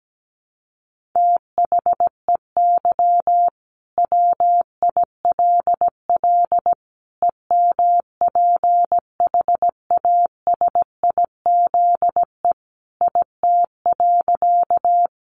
Morse code, heard as a keyed tone, about 17 wpm, 700 Hz. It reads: THEY WILL EMPHASIZE IT.
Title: Sentences from Top 1500 Words 17wpm